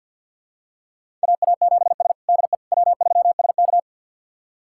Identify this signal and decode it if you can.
Morse code, heard as a keyed tone, about 46 wpm, 700 Hz. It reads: AA7S DE W3SK
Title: USA Callsign DE Callsign - 4 Characters - 50wpm